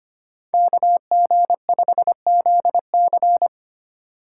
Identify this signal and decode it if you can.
Morse code, heard as a keyed tone, about 32 wpm, 700 Hz. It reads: KG5ZC